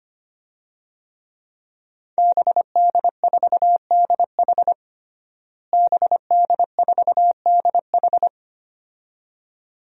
Morse code, heard as a keyed tone, about 25 wpm, 700 Hz. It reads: BD4DH BD4DH
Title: DXCC-100 Call Signs - 5 Characters - Repeated 2x - 25wpm